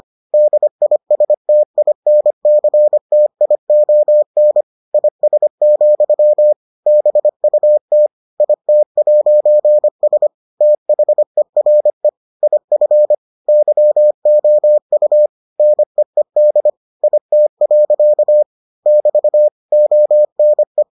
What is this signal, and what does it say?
Morse code, heard as a keyed tone, about 25 wpm, 600 Hz. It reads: DISTINCTION IS, BUT IT'S THERE IF YOU NEED IT. <BT> ONE